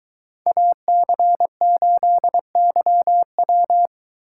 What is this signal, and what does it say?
Morse code, heard as a keyed tone, about 23 wpm, 700 Hz. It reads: AC8YW